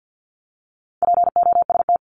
Morse code, heard as a keyed tone, about 50 wpm, 700 Hz. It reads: POST
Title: Top 751-1000 Words 50wpm